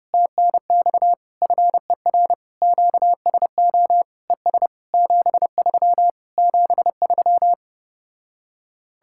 Morse code, heard as a keyed tone, about 33 wpm, 700 Hz. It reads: TNX FER QSO ES 73 73